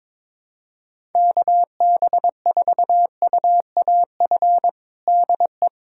Morse code, heard as a keyed tone, about 22 wpm, 700 Hz. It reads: KB4UAF DE
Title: USA Callsign DE Callsign 22wpm